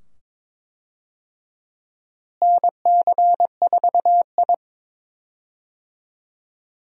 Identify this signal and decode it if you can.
Morse code, heard as a keyed tone, about 22 wpm, 700 Hz. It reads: NC4I
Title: USA Call Signs - 4 Characters - Repeated 2x - 22wpm